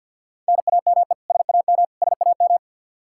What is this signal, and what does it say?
Morse code, heard as a keyed tone, about 48 wpm, 700 Hz. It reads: NAME SAM SAM
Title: Practice QSO - Exchange 1 - Beginning - 50wpm